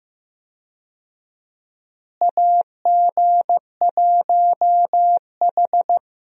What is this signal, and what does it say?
Morse code, heard as a keyed tone, about 15 wpm, 700 Hz. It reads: AG1H
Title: USA Call Signs - 4 Characters - Repeated 2x - 15wpm